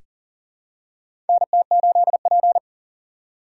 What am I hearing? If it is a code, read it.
Morse code, heard as a keyed tone, about 39 wpm, 690 Hz. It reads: NT8P
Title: USA Call Signs - 4 Characters - 2x Character Spacing - 40wpm